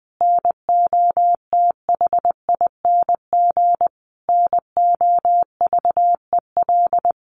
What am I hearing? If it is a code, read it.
Morse code, heard as a keyed tone, about 20 wpm, 700 Hz. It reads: NOTHING NOVEL
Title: Sentences from Top 1300 Words 20wpm